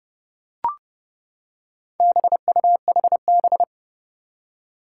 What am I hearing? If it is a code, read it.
Morse code, heard as a keyed tone, about 30 wpm, 700 Hz. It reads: BUHB
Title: Four Challenging Letter-Number Combinations 30wpm